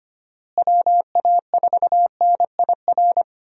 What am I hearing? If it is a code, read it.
Morse code, heard as a keyed tone, about 25 wpm, 700 Hz. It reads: WA4NIR
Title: USA Call Signs - 6 Characters - Repeated 3x - 25wpm